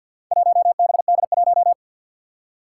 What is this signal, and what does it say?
Morse code, heard as a keyed tone, about 48 wpm, 700 Hz. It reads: 1BD1
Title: Four Challenging Letter-Number Combinations 50wpm